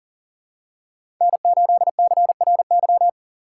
Morse code, heard as a keyed tone, about 40 wpm, 700 Hz. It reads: N8CRY
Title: USA Call Signs - 5 Characters - 4x Character Spacing - 40wpm